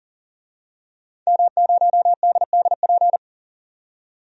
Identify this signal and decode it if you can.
Morse code, heard as a keyed tone, about 40 wpm, 700 Hz. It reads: M0DDP
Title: DXCC-100 Call Signs - 5 Characters - 40wpm